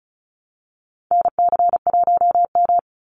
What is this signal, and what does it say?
Morse code, heard as a keyed tone, about 34 wpm, 700 Hz. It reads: NC1M
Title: Call Signs Repeated 2x - 35wpm